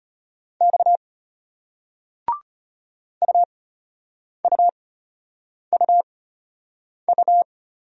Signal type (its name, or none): Morse code